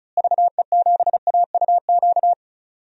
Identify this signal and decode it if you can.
Morse code, heard as a keyed tone, about 36 wpm, 700 Hz. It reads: VE7AUQ
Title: DXCC-100 Call Signs - 6 Characters - Repeated 3x - 35wpm